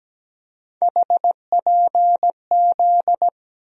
Morse code, heard as a keyed tone, about 17 wpm, 700 Hz. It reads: HPZ
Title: USA Call Sign Postfixes - 3 Letters 17wpm